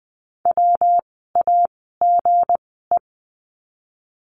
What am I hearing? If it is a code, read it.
Morse code, heard as a keyed tone, about 20 wpm, spaced as for 14 wpm, 700 Hz. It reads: WAGE
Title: 160 Lesson 26f - Letter g - Words 20wpm